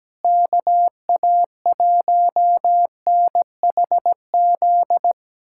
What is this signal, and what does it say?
Morse code, heard as a keyed tone, about 17 wpm, 700 Hz. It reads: KA1NHZ
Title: USA Callsign DE Callsign 17wpm